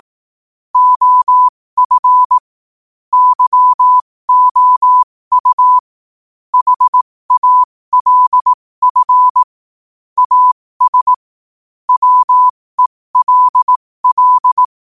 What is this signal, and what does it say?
Morse code, heard as a keyed tone, about 18 wpm, spaced as for 15 wpm, 1000 Hz. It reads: OF YOU HALF AS WELL
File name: qotdfiles / 15wpm